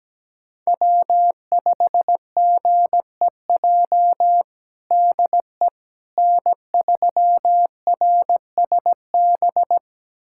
Morse code, heard as a keyed tone, about 17 wpm, 700 Hz. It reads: W5GEJ DE N3RSB
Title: USA Callsign DE Callsign - 5 Characters - 17wpm